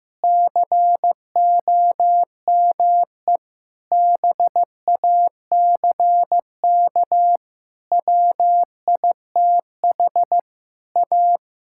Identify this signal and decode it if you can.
Morse code, heard as a keyed tone, about 15 wpm, 700 Hz. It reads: COME BACK WITH A
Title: Sentences from Top 1000 Words - Review Part 6 - 15wpm